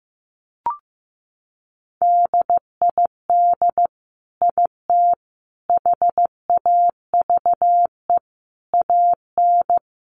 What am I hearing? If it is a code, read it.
Morse code, heard as a keyed tone, about 15 wpm, 700 Hz. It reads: DID IT HAVE AN
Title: Sentences from Top 1200 Words 15wpm